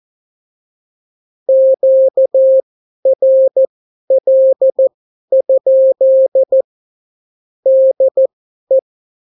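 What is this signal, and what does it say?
Morse code, heard as a keyed tone, about 14 wpm, spaced as for 11 wpm, 530 Hz. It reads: QRL? DE
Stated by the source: Example Spanish CW QSO from a Morse learning book, 14 wpm characters with 8 wpm spacing.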